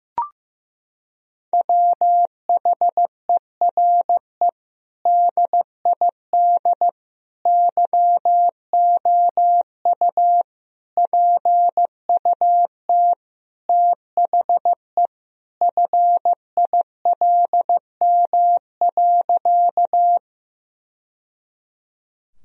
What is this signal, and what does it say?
Morse code, heard as a keyed tone, about 15 wpm, 700 Hz. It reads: WHERE DID YOU PUT THE FILM.
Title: Sentences from Top 1000 Words - Review Part 3 - 15wpm